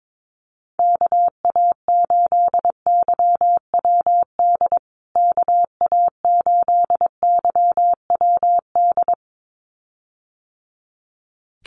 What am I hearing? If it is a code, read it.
Morse code, heard as a keyed tone, about 22 wpm, 700 Hz. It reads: KA8YWD KA8YWD
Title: USA Call Signs Repeated 2x 22wpm